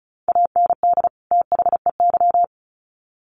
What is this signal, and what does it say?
Morse code, heard as a keyed tone, about 34 wpm, 700 Hz. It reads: AND THEY
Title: Sets of 2 Words 35wpm